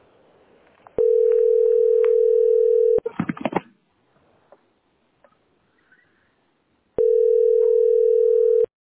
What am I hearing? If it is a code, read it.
Ringback tone (North America), 440 + 480 Hz.